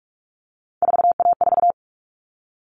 Morse code, heard as a keyed tone, about 44 wpm, 700 Hz. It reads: 4A4
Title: All Cut Numbers - 3-Digit 45wpm